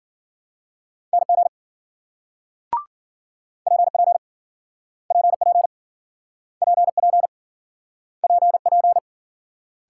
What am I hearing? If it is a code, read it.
Morse code, heard as a keyed tone, about 53 wpm, 700 Hz. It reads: NG PP PP PP PP